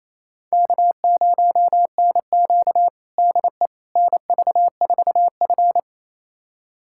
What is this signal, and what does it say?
Morse code, heard as a keyed tone, about 28 wpm, 700 Hz. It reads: K0NQ DE NV4F
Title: USA Callsign DE Callsign - 4 Characters - 28wpm